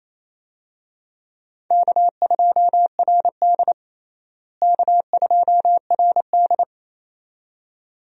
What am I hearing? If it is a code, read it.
Morse code, heard as a keyed tone, about 27 wpm, 700 Hz. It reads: K2RD K2RD
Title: USA Call Signs - 4 Characters - Repeated 2x - 28wpm